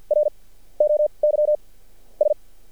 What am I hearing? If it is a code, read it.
Morse code, heard as a keyed tone, about 50 wpm, spaced as for 17 wpm, 600 Hz. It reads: ROYS
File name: archived / CRSnet